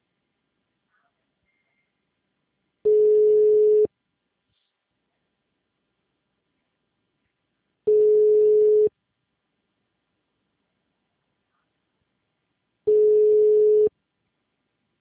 Ringback tone (Europe, 425 Hz).
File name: out- -2059- - - .68739